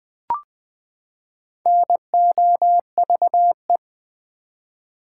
Morse code, heard as a keyed tone, about 20 wpm, 700 Hz. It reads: NOVE
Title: Build A Word - 4 Letters At A Time - 8-Letter Words - 20wpm